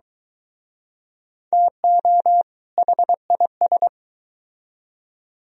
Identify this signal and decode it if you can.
Morse code, heard as a keyed tone, about 23 wpm, 700 Hz. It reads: TO HIS